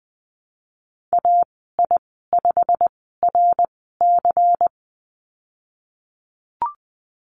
Morse code, heard as a keyed tone, about 20 wpm, spaced as for 14 wpm, 700 Hz. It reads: AI5RC
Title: 140 Lesson 23g - Letter f - Call Signs 20wpm